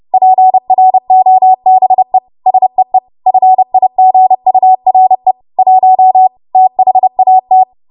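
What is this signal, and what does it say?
Morse code, heard as a keyed tone, about 30 wpm, 750 Hz. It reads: PROBE SEE FIGURE 1 THAT
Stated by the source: W1AW Web Code Practice Files, 30 WPM run from ARRL; text from April 2023 QST.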